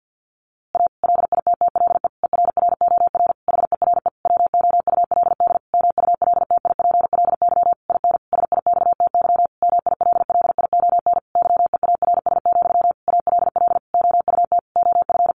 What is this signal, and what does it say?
Morse code, heard as a keyed tone, about 50 wpm, 700 Hz. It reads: A LITTLE ERROR HERE WOULD MULTIPLY IN SIXTY MILLION YEARS, ALL OUT OF
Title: A Sound Of Thunder 50wpm